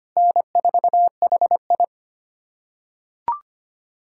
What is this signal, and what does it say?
Morse code, heard as a keyed tone, about 25 wpm, 700 Hz. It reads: N4HI
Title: USA Callsign DE Callsign - 4 Characters - 25wpm